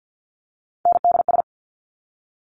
Morse code, heard as a keyed tone, about 50 wpm, 700 Hz. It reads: NDS
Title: Top Three Letter Combinations in Words 50wpm